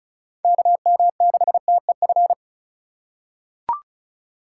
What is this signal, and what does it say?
Morse code, heard as a keyed tone, about 36 wpm, 700 Hz. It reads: KM6TEF
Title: USA Call Signs - 3x Character Spacing - 35wpm